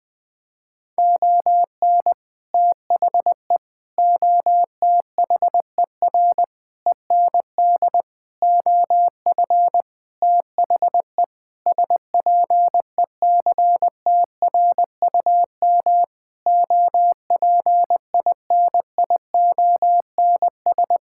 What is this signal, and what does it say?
Morse code, heard as a keyed tone, about 20 wpm, 700 Hz. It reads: ON THE OTHER END OF THE SPECTRUM OPINIONS
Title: Sentences with 7-Word Phrases - 20wpm